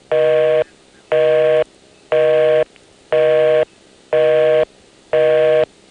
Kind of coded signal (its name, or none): telephone call progress tone